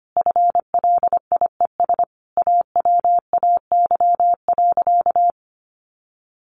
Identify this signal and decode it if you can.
Morse code, heard as a keyed tone, about 25 wpm, 700 Hz. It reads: FLIES AWAY.
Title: Additional Easy Phrases 25wpm